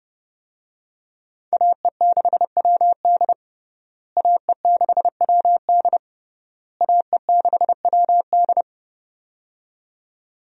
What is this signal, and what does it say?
Morse code, heard as a keyed tone, about 30 wpm, 700 Hz. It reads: AE6WD AE6WD AE6WD
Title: USA Call Signs - 5 Characters - Repeated 3x - 30wpm